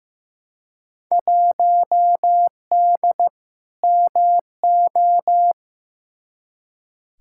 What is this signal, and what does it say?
Morse code, heard as a keyed tone, about 15 wpm, 700 Hz. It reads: 1D MO
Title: Field Day Exchanges - 15wpm